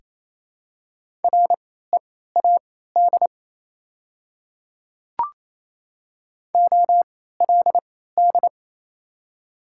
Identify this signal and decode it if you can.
Morse code, heard as a keyed tone, about 28 wpm, spaced as for 16 wpm, 700 Hz. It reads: READ OLD
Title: 3x Character Spacing - Top 200 Words - 28wpm